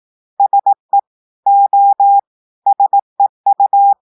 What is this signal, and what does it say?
Morse code, heard as a keyed tone, about 18 wpm, 800 Hz. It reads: SE O SEU
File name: portugais texte en clair 2 vitesse 18 mots minutes